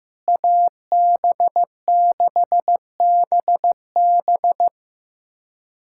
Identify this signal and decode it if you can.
Morse code, heard as a keyed tone, about 15 wpm, 700 Hz. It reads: AB6BB
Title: USA Callsign DE Callsign - 5 Characters - 15wpm